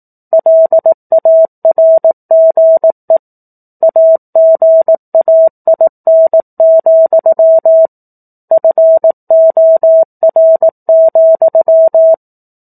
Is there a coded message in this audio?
Yes: Morse code from 0.3 to 12.1 s, heard as a keyed tone, about 18 wpm, 650 Hz. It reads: LARGE AGAIN, FOR,